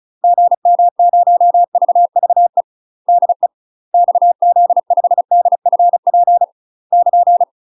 Morse code, heard as a keyed tone, about 36 wpm, 700 Hz. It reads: GM0VVE DE XZ5DFP <KN>
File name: UK MASTER QSOs TRAINING / 35WPM / UK MASTER QSOs 0038